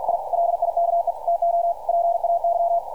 Morse code, heard as a keyed tone, about 48 wpm, 690 Hz, with radio noise. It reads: I LIKE TO PAY
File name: archived / CRSnet